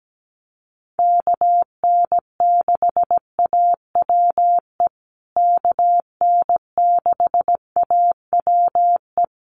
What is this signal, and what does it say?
Morse code, heard as a keyed tone, about 17 wpm, 700 Hz. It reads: KN6AWE KN6AWE
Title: USA Call Signs Repeated 3x 17wpm